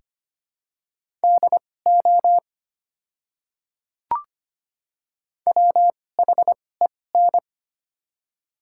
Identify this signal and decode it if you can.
Morse code, heard as a keyed tone, about 25 wpm, spaced as for 18 wpm, 700 Hz. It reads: DO WHEN
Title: 2x Character Spacing - Top 300 Words - 25wpm